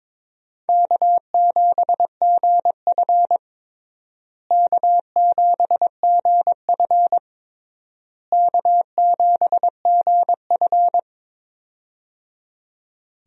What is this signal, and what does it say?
Morse code, heard as a keyed tone, about 22 wpm, 700 Hz. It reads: K7GF K7GF K7GF
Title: USA Call Signs - 4 Characters - Repeated 3x - 22wpm